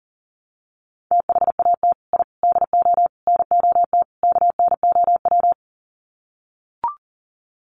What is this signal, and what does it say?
Morse code, heard as a keyed tone, about 40 wpm, 700 Hz. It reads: THAT I DO NOT KNOW
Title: Sets of 5 Words Encore2 40wpm